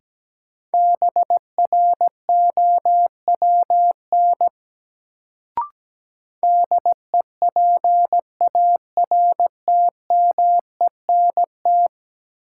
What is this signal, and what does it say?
Morse code, heard as a keyed tone, about 17 wpm, 700 Hz. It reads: BROWN DEPARTMENT
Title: Sentences from Top 1200 Words - New Word Drill Alt - 17wpm